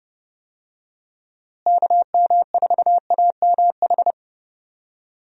Morse code, heard as a keyed tone, about 32 wpm, 700 Hz. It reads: KM4AMH
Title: USA Call Signs - 6 Characters - Repeated 3x - 30wpm